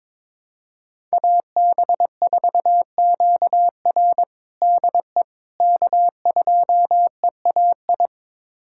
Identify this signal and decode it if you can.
Morse code, heard as a keyed tone, about 22 wpm, 700 Hz. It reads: AB4QR DE K2EAI
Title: USA Callsign DE Callsign - 5 Characters - 22wpm